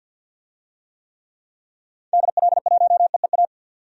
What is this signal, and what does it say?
Morse code, heard as a keyed tone, about 50 wpm, 700 Hz. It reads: DL1EEA
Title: DXCC-100 Call Signs - 6 Characters - 50wpm